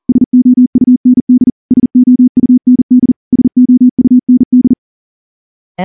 Morse code, heard as a keyed tone, about 38 wpm, 250 Hz. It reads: SOUND SOUND SOUND
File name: morse-training-40WPM